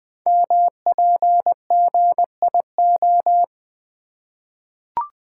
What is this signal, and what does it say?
Morse code, heard as a keyed tone, about 20 wpm, 700 Hz. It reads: MPGIO
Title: Code Groups - 5 Letters - 20wpm